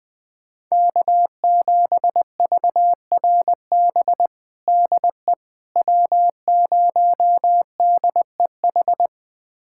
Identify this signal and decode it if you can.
Morse code, heard as a keyed tone, about 21 wpm, 700 Hz. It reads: K7VRB DE W0DEH
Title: USA Callsign DE Callsign - 5 Characters - 20wpm